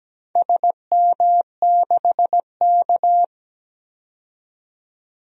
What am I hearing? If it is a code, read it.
Morse code, heard as a keyed tone, about 17 wpm, 700 Hz. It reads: SM6K
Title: DXCC-100 Call Signs - 4 Characters - 17wpm